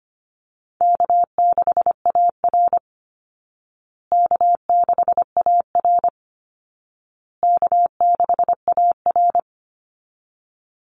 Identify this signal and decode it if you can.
Morse code, heard as a keyed tone, about 25 wpm, 700 Hz. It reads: K6AR K6AR K6AR
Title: Easy Call Signs Repeated 3x - 25wpm